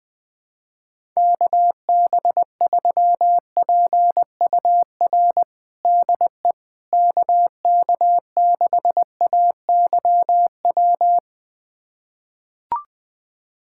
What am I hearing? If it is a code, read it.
Morse code, heard as a keyed tone, about 20 wpm, 700 Hz. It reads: KB3PUR DE KK6AYW
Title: USA Callsign DE Callsign - 6 Characters - 20wpm